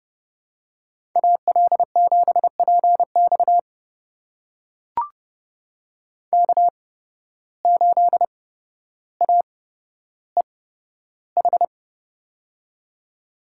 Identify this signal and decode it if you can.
Morse code, heard as a keyed tone, about 30 wpm, spaced as for 13 wpm, 700 Hz. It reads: AL7PX K8AEH
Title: USA Call Signs - 8x Character Spacing - 30wpm